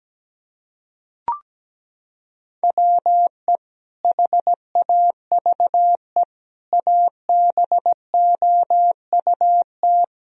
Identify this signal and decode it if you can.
Morse code, heard as a keyed tone, about 17 wpm, 700 Hz. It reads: WE HAVE ABOUT
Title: Sentences from Top 1000 Words - Review Part 1 - 17wpm